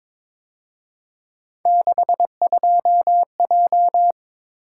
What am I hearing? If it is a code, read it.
Morse code, heard as a keyed tone, about 26 wpm, 700 Hz. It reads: 62J
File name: Speed-Racing - Three Letter-Number Combinations - 22wpm